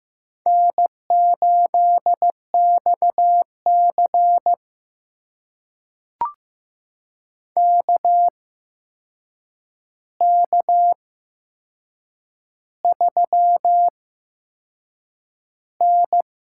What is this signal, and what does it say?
Morse code, heard as a keyed tone, about 15 wpm, spaced as for 6 wpm, 700 Hz. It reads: N8XC KK3N
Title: USA Call Signs - 4 Characters - 8x Character Spacing - 15wpm